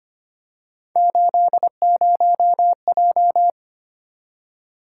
Morse code, heard as a keyed tone, about 25 wpm, 700 Hz. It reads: 80J
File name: Three Letter-Number Combinations - 25wpm